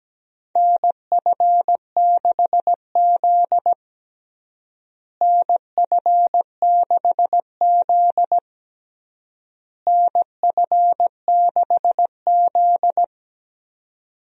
Morse code, heard as a keyed tone, about 17 wpm, 700 Hz. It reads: NF6Z NF6Z NF6Z
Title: USA Call Signs - 4 Characters - Repeated 3x - 17wpm